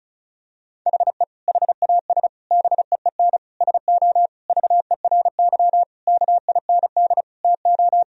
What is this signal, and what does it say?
Morse code, heard as a keyed tone, about 35 wpm, 700 Hz. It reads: HE HAS BEEN SO VERY KIND TO
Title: Sentences from Top 1000 Words - Review Part 1 - 35wpm